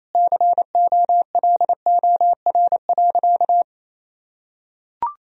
Morse code, heard as a keyed tone, about 28 wpm, 700 Hz. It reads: COLOR.
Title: Sentences from Top 1000 Words - Review Part 2 - 28wpm